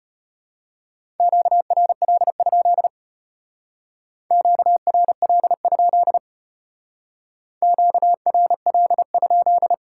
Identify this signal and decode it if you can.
Morse code, heard as a keyed tone, about 30 wpm, 700 Hz. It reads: QRL? QRL? QRL?
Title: Speed-Racing Top 90 QSO Elements 25wpm